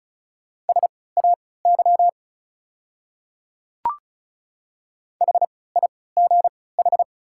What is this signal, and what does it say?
Morse code, heard as a keyed tone, about 34 wpm, spaced as for 20 wpm, 700 Hz. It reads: SAY HIGH
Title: 3x Character Spacing - Top 300 Words - 35wpm